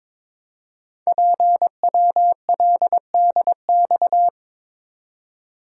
Morse code, heard as a keyed tone, about 22 wpm, 700 Hz. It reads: PWLDX
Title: Code Groups - 5 Letter:Numbers - 22wpm